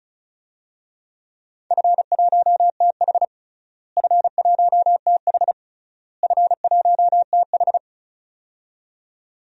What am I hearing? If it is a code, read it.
Morse code, heard as a keyed tone, about 36 wpm, 700 Hz. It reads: F1TH F1TH F1TH
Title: DXCC-100 Call Signs - 4 Characters - Repeated 3x - 35wpm